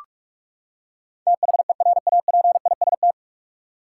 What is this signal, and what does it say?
Morse code, heard as a keyed tone, about 46 wpm, 700 Hz. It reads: THERAPIST
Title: K1USN SST QSO Practice 45wpm